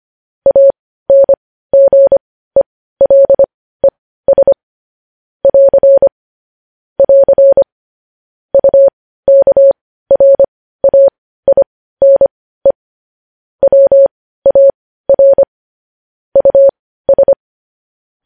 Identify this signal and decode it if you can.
Morse code, heard as a keyed tone, about 25 wpm, spaced as for 15 wpm, 550 Hz. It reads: ANGELES <AR> <AR> UKRAINE WAR US
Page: news-headlines-in-morse-code-at-15-wpm